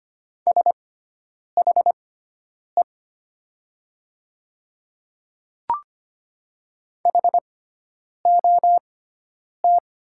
Morse code, heard as a keyed tone, about 25 wpm, spaced as for 9 wpm, 700 Hz. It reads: SHE HOT
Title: 6x Character Spacing - Top 200 Words - 25wpm